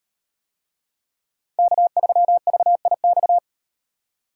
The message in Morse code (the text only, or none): K3VIX